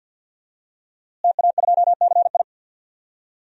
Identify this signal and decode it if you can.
Morse code, heard as a keyed tone, about 52 wpm, 700 Hz. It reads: TA2CI
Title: DXCC-100 Call Signs - 5 Characters - Repeated 2x - 50wpm